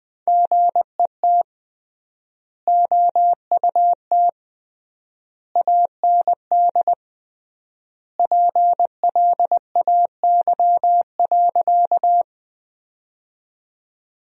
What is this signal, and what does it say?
Morse code, heard as a keyed tone, about 21 wpm, 700 Hz. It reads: GET OUT AND PLAY.
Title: 3x Word-Spacing - Sentences from Top 200 Words - 20wpm